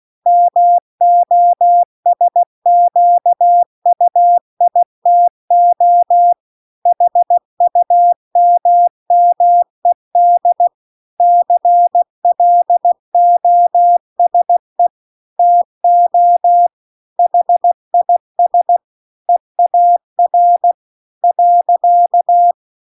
Morse code, heard as a keyed tone, about 16 wpm, 700 Hz. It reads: MOSQUITO HUMMED CLOSE TO HIS EAR .